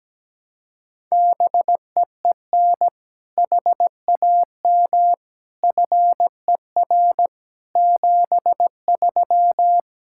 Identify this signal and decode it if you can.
Morse code, heard as a keyed tone, about 17 wpm, 700 Hz. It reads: BEEN HAM FER 73
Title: K1USN SST QSO Practice 17wpm